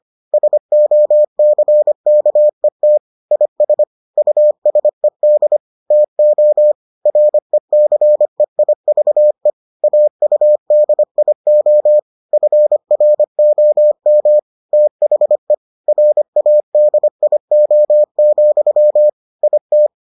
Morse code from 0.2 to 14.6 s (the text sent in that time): SOCKET IS USED TO RECEIVE AUDIO FROM